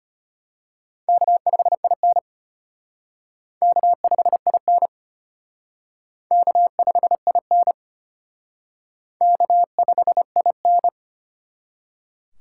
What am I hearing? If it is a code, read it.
Morse code, heard as a keyed tone, about 32 wpm, 700 Hz. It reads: K5IN K5IN K5IN K5IN